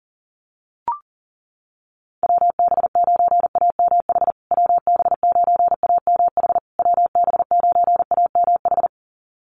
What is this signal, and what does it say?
Morse code, heard as a keyed tone, about 40 wpm, 700 Hz. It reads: WB9AMH WB9AMH WB9AMH
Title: USA Call Signs Repeated 3x 40wpm